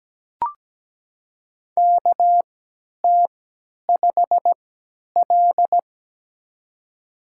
Morse code, heard as a keyed tone, about 17 wpm, spaced as for 10 wpm, 700 Hz. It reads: KT5L
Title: USA Call Signs - 4 Characters - 3x Character Spacing - 17wpm